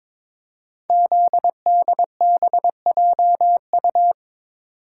Morse code, heard as a keyed tone, about 22 wpm, 700 Hz. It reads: ZDBJU
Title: Code Groups - 5 Letters - 22wpm